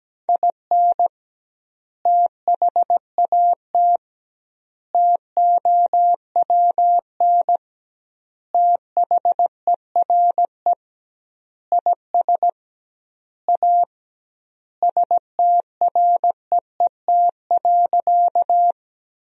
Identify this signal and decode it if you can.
Morse code, heard as a keyed tone, about 17 wpm, 700 Hz. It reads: IN THAT TOWN THERE IS A STREET.
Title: 2x Word-Spacing - Sentences from Top 500 Words - 17wpm